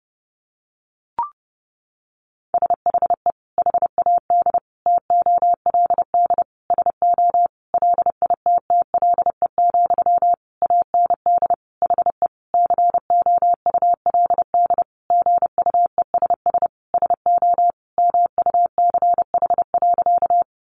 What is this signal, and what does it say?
Morse code, heard as a keyed tone, about 30 wpm, 700 Hz. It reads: SHE HAD TOLD SO LITTLE, AND HE COULD GUESS SO MUCH.
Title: Sentences from Top 1500 Words 30wpm